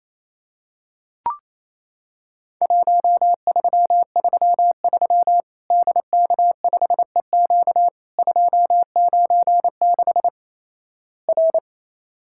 Morse code, heard as a keyed tone, about 28 wpm, 700 Hz. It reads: 1333 DK5EQ 296 R
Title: Worked All Europe DX Contest - QTC Exchange - 28wpm